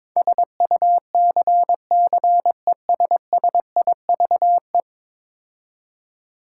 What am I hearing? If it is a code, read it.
Morse code, heard as a keyed tone, about 21 wpm, 700 Hz. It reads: SUCCESSIVE
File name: practice / 22wpm / 10 Letter Words - First 3 Letters Given - 22wpm